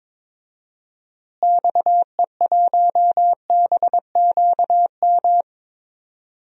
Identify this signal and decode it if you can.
Morse code, heard as a keyed tone, about 22 wpm, 700 Hz. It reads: XE1BQM
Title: DXCC-100 Call Signs - 6 Characters - Repeated 2x - 22wpm